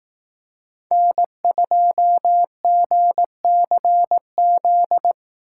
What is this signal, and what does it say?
Morse code, heard as a keyed tone, about 18 wpm, 700 Hz. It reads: N2GCZ
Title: Speed Racing - USA 1x3 Call Signs 15wpm